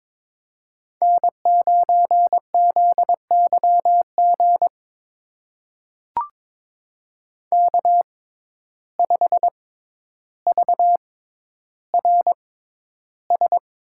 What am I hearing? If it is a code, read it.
Morse code, heard as a keyed tone, about 22 wpm, spaced as for 11 wpm, 700 Hz. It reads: N9ZYG K5VRS
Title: USA Call Signs - 5 Characters - 6x Character Spacing - 22wpm